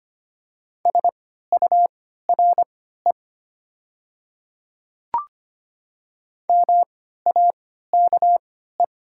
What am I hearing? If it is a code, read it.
Morse code, heard as a keyed tone, about 25 wpm, spaced as for 14 wpm, 700 Hz. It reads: SURE MAKE
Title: 3x Character Spacing - Top 300 Words - 25wpm